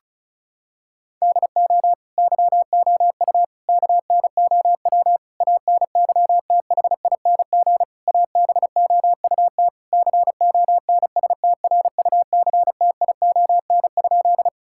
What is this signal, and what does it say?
Morse code, heard as a keyed tone, about 36 wpm, 700 Hz. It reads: DO YOU KNOW ANYTHING ABOUT CONSTRUCTION?
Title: Sentences from Top 1600 Words 35wpm